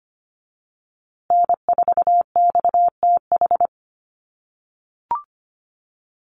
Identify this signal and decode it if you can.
Morse code, heard as a keyed tone, about 25 wpm, 700 Hz. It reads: N4XTH
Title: USA Call Signs Repeated 3x 25wpm